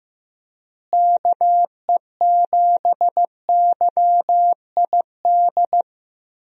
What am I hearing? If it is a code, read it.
Morse code, heard as a keyed tone, about 15 wpm, 700 Hz. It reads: KE7YID
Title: USA Call Signs - 6 Characters - Repeated 3x - 15wpm